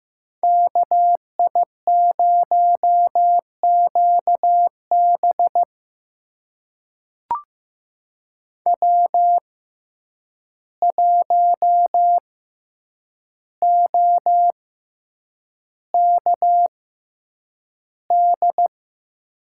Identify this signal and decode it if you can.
Morse code, heard as a keyed tone, about 15 wpm, spaced as for 8 wpm, 700 Hz. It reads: KI0QB W1OKD
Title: USA Call Signs - 5 Characters - 6x Character Spacing - 15wpm